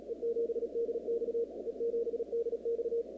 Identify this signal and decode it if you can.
Morse code, heard as a keyed tone, about 36 wpm, 470 Hz. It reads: E7DX E7DX
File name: contests / 2019-cw-wpx-kt5j-k5pi- -2330 0-37